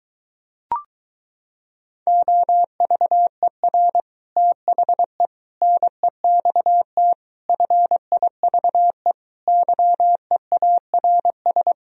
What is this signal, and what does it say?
Morse code, heard as a keyed tone, about 24 wpm, 700 Hz. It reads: OVER THE NEXT FIVE YEARS